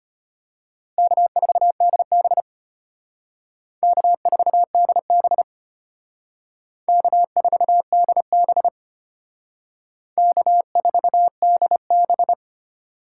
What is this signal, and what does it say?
Morse code, heard as a keyed tone, about 32 wpm, 700 Hz. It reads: K4DB K4DB K4DB K4DB